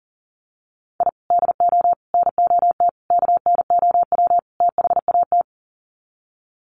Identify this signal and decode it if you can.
Morse code, heard as a keyed tone, about 40 wpm, 700 Hz. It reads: I DO NOT KNOW THAT
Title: Sets of 5 Words 40wpm